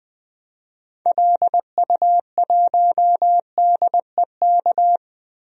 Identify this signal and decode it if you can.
Morse code, heard as a keyed tone, about 20 wpm, 700 Hz. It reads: LU1DEK
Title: DXCC-100 Call Signs - 6 Characters - Repeated 3x - 20wpm